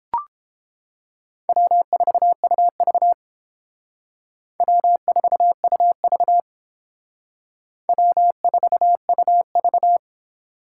Morse code, heard as a keyed tone, about 28 wpm, 700 Hz. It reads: W4UV W4UV W4UV